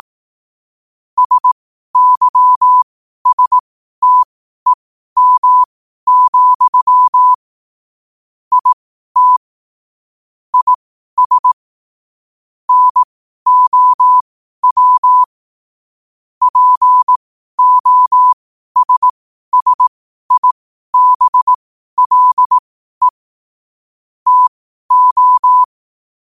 Morse code, heard as a keyed tone, about 18 wpm, spaced as for 12 wpm, 1000 Hz. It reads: SYSTEM, IT IS NOW POSSIBLE TO
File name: qotdfiles / 12wpm